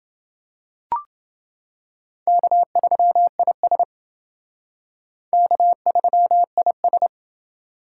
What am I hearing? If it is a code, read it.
Morse code, heard as a keyed tone, about 29 wpm, 700 Hz. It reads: K3IS K3IS